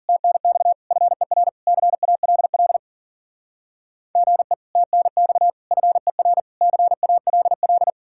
Morse code, heard as a keyed tone, about 48 wpm, 700 Hz. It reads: TNX FER CALL GE TNX FER CALL
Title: Speed-Racing Common QSO Phrases - 40wpm